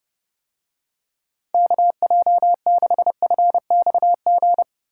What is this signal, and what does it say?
Morse code, heard as a keyed tone, about 31 wpm, 700 Hz. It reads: KJ6FXG